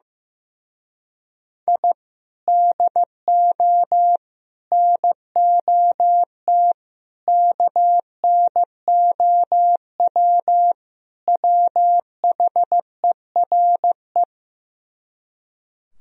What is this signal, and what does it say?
Morse code, heard as a keyed tone, about 15 wpm, 700 Hz. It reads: I DO NOT KNOW WHERE
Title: Speed-Racing - Sets of 5 Words 15wpm